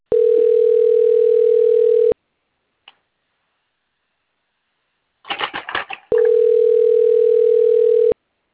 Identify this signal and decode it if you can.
Ringback tone (North America), 440 + 480 Hz.